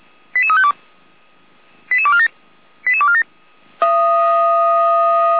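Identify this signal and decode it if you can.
Five-tone selective call (ZVEI), heard as quick runs of five tones, tones of about 70 ms. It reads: 88352 88247 88247
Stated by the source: SDR-received ZVEI five-tone selective call, code 88102, recorded by feuerwehr-orlamuende.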